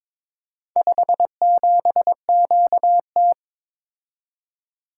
Morse code, heard as a keyed tone, about 21 wpm, 700 Hz. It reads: 57QT
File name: practice / 22wpm / Four Letter-Number Combinations - 22wpm